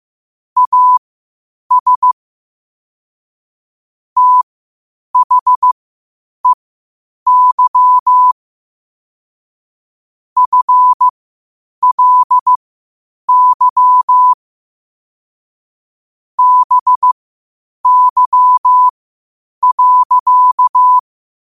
Morse code, heard as a keyed tone, about 15 wpm, spaced as for 8 wpm, 1000 Hz. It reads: AS THEY FLY BY.
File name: qotdfiles / 8wpm